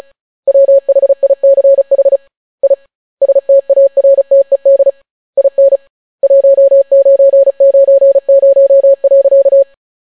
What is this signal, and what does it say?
Morse code, heard as a keyed tone, about 35 wpm, 550 Hz. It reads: WHICH I STARTED IN 1990.